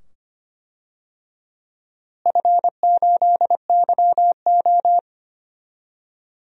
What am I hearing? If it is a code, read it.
Morse code, heard as a keyed tone, about 25 wpm, 700 Hz. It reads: F8YO
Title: DXCC-100 Call Signs - 4 Characters - Repeated 2x - 25wpm